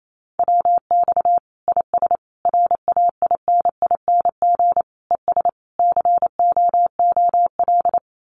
Morse code, heard as a keyed tone, about 28 wpm, 700 Hz. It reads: WX IS RAINING ES COOL
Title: Common QSO Phrases 28wpm